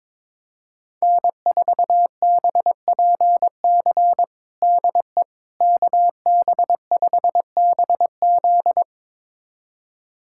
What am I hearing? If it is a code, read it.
Morse code, heard as a keyed tone, about 22 wpm, 700 Hz. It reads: N4BPC DE KB5BZ
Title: USA Callsign DE Callsign - 5 Characters - 22wpm